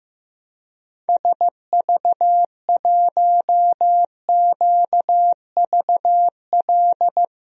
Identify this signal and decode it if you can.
Morse code, heard as a keyed tone, about 15 wpm, 700 Hz. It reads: SV1QVL
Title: DXCC-100 Call Signs - 6 Characters - Repeated 2x - 15wpm